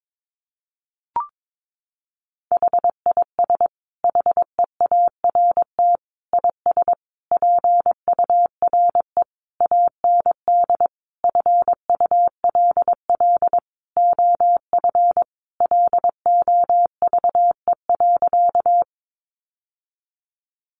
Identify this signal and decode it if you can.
Morse code, heard as a keyed tone, about 22 wpm, 700 Hz. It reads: HIS HEART IS PURE AND FULL OF LOVE.
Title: Sentences from Top 1500 Words 22wpm